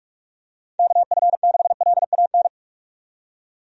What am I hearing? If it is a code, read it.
Morse code, heard as a keyed tone, about 46 wpm, 700 Hz. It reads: KF6LAN
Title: USA Call Signs - 6 Characters - 45wpm